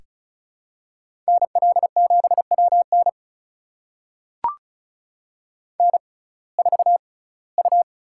Morse code, heard as a keyed tone, about 36 wpm, spaced as for 22 wpm, 700 Hz. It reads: NL7WN N4U
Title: USA Call Signs - 5 Characters - 6x Character Spacing - 35wpm